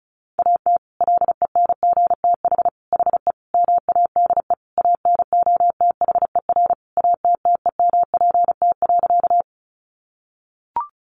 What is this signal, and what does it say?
Morse code, heard as a keyed tone, about 34 wpm, 700 Hz. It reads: AT LENGTH HE MADE ANOTHER ATTEMPT.
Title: Sentences from Top 1000 Words 35wpm